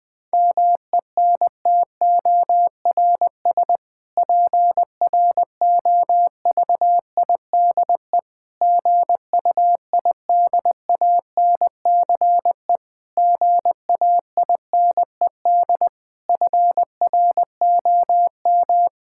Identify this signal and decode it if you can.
Morse code, heard as a keyed tone, about 20 wpm, 700 Hz. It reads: MENTORS PROVIDE GUIDANCE GAINED FROM
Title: Difficult Sentences with Hints - 20wpm